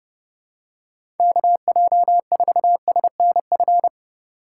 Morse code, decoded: KJ4SNF

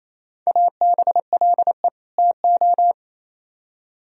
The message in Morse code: ABLE TO

